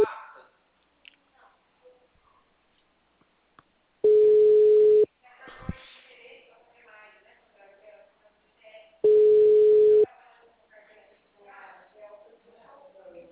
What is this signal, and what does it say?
Ringback tone (Europe, 425 Hz), with speech.